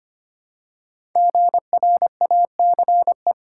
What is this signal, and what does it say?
Morse code, heard as a keyed tone, about 25 wpm, 700 Hz. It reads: GRACE
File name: 5 Letter Words - First 3 Letters Given - 25wpm